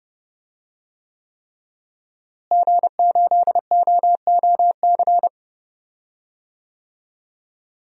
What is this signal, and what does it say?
Morse code, heard as a keyed tone, about 30 wpm, 700 Hz. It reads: G8OOC
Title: DXCC-100 Call Signs - 5 Characters - Repeated 3x - 30wpm